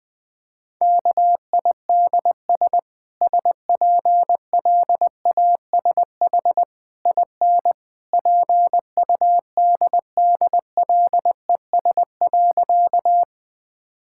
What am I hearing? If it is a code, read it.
Morse code, heard as a keyed tone, about 20 wpm, 700 Hz. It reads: KIDS SPLASH IN PUDDLES.
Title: Easy Sentences with Hints - 20wpm